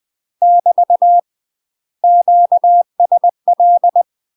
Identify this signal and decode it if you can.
Morse code, heard as a keyed tone, about 20 wpm, 700 Hz. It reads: <BT> QSL
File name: UK MASTER QSOs TRAINING / 20WPM / UK MASTER QSOs 0018